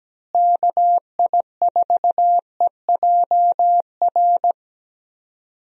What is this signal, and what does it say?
Morse code, heard as a keyed tone, about 17 wpm, 700 Hz. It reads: KI4EJR
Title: USA Call Signs - 6 Characters - Repeated 3x - 17wpm